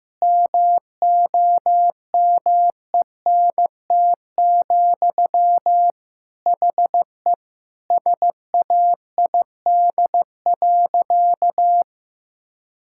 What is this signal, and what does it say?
Morse code, heard as a keyed tone, about 16 wpm, 700 Hz. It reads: MOMENT, HE SAID.
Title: Sentences from Top 1000 Words - Review Part 5 - 15wpm